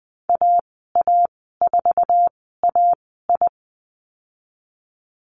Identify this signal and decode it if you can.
Morse code, heard as a keyed tone, about 20 wpm, spaced as for 15 wpm, 700 Hz. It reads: AA4AI
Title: 069 Lesson 12g - Letter d - Call Signs 20wpm